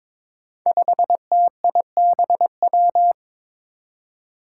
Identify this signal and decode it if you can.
Morse code, heard as a keyed tone, about 22 wpm, 700 Hz. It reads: 5TIBW
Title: Five Challenging Letter-Number Combinations 22wpm